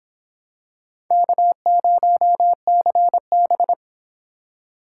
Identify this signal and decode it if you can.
Morse code, heard as a keyed tone, about 24 wpm, 700 Hz. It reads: K0CB